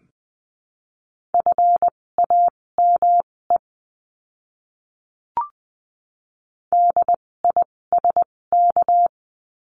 Morse code, heard as a keyed tone, about 20 wpm, spaced as for 16 wpm, 700 Hz. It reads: FAME DISK
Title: 256 Lesson 42a - Words 12-20wpm